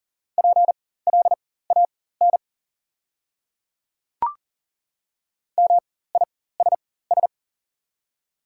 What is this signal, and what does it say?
Morse code, heard as a keyed tone, about 40 wpm, spaced as for 19 wpm, 700 Hz. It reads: PLAN MISS
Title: 4x Character Spacing - Top 500 Words - 40wpm